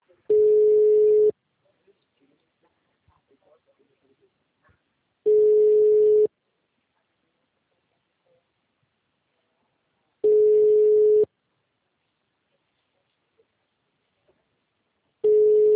Ringback tone (Europe, 425 Hz).